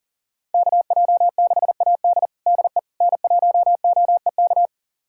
Morse code, heard as a keyed tone, about 40 wpm, 700 Hz. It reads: KJ6AD DE N1OEK
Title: USA Callsign DE Callsign - 5 Characters - 40wpm